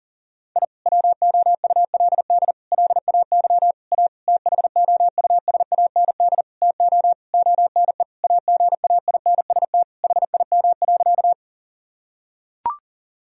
Morse code, heard as a keyed tone, about 40 wpm, 700 Hz. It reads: I WOULD LAY A THOUSAND TO ONE AGAINST HIM.
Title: Sentences from Top 1000 Words - Review Part 3 - 40wpm